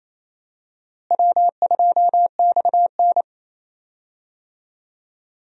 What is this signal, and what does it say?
Morse code, heard as a keyed tone, about 28 wpm, 700 Hz. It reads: W2XN
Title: USA Call Signs - 4 Characters - Repeated 2x - 28wpm